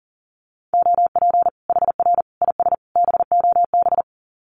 Morse code, heard as a keyed tone, about 40 wpm, 700 Hz. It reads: OP HR IS BOB
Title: Common QSO Phrases 40wpm